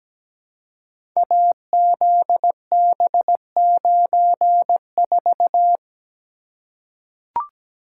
Morse code, heard as a keyed tone, about 17 wpm, 700 Hz. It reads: AZB94
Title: Code Groups - 5 Letter:Numbers - 17wpm